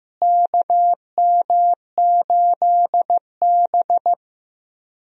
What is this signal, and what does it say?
Morse code, heard as a keyed tone, about 15 wpm, 700 Hz. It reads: KM8B
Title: USA Call Signs - 4 Characters - Repeated 2x - 15wpm